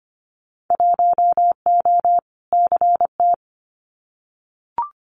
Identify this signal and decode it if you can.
Morse code, heard as a keyed tone, about 25 wpm, 700 Hz. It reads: 1O CT
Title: Winter Field Day Exchanges Encore 25wpm